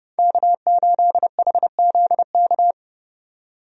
Morse code, heard as a keyed tone, about 31 wpm, 700 Hz. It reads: K8HZK
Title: USA Call Signs - 5 Characters - 30wpm